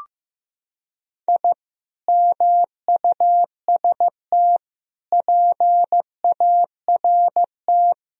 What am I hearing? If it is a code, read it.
Morse code, heard as a keyed tone, about 15 wpm, 700 Hz. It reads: I MUST PART
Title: Sentences from Top 1000 Words - Review Part 1 - 15wpm